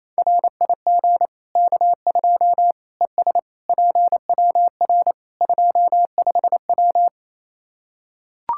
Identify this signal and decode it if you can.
Morse code, heard as a keyed tone, about 28 wpm, 700 Hz. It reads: RIG K2 ES PWR 25W
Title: Practice QSO - Exchange 2 - Info - 28wpm